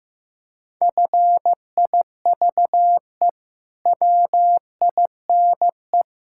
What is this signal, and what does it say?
Morse code, heard as a keyed tone, about 15 wpm, 700 Hz. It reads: FIVE WINE
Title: Pangram Challenge v2 - 15wpm